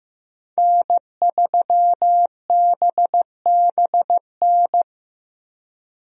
Morse code, heard as a keyed tone, about 15 wpm, 700 Hz. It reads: N3BBN
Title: USA Call Signs - 5 Characters - Repeated 2x - 15wpm